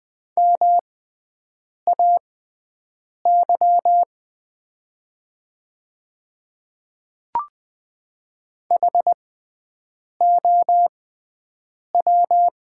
Morse code, heard as a keyed tone, about 20 wpm, spaced as for 7 wpm, 700 Hz. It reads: MAY HOW
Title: 6x Character Spacing - Top 100 Words - 20wpm